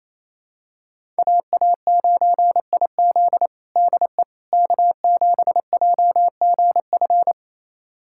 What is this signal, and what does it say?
Morse code, heard as a keyed tone, about 28 wpm, 700 Hz. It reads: AA9IZ DE K7JGF
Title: USA Callsign DE Callsign - 5 Characters - 28wpm